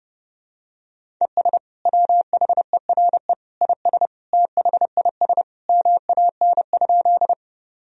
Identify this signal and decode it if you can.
Morse code, heard as a keyed tone, about 32 wpm, 700 Hz. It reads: ES WHERE IS THIS MAN?
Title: Sentences from Top 200 Words and Abbreviations 30wpm